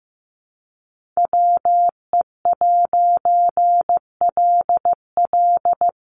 Morse code, heard as a keyed tone, about 14 wpm, 700 Hz. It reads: WE'LL
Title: A Sound Of Thunder 15wpm